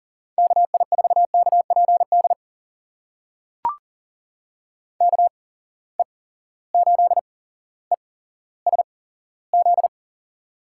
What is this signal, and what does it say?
Morse code, heard as a keyed tone, about 40 wpm, spaced as for 17 wpm, 700 Hz. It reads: KI4KPD KE8ESZ
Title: USA Call Signs - 6 Characters - 8x Character Spacing - 40wpm